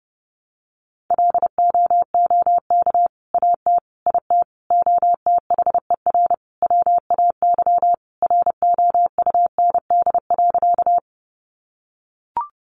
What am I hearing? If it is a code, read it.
Morse code, heard as a keyed tone, about 30 wpm, 700 Hz. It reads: LOOK AT IT OTHER WAY ROUND.
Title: Sentences from Top 200 Words 30wpm